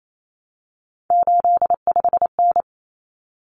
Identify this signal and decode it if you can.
Morse code, heard as a keyed tone, about 28 wpm, 700 Hz. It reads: 85N
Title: Common Cut Numbers - 3-Digit 28wpm